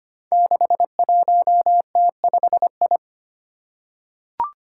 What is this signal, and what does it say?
Morse code, heard as a keyed tone, about 25 wpm, 700 Hz. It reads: 61T5I
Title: Five Challenging Letter-Number Combinations 25wpm